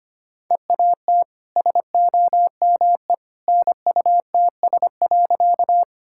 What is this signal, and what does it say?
Morse code, heard as a keyed tone, about 25 wpm, 700 Hz. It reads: EAT SOME NUTS.
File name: Easy Sentences with Hints - 25wpm